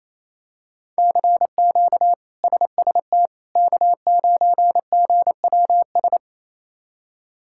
Morse code, heard as a keyed tone, about 28 wpm, 700 Hz. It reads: CQ SST K9GWS
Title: K1USN SST QSO Practice - 28wpm